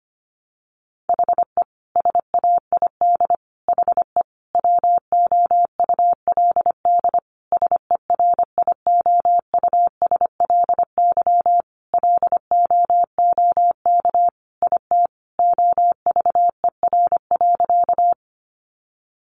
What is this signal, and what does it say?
Morse code, heard as a keyed tone, about 25 wpm, 700 Hz. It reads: HE SAID HE WOULD SERIOUSLY LOOK IT OVER.
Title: Sentences from Top 1200 Words 25wpm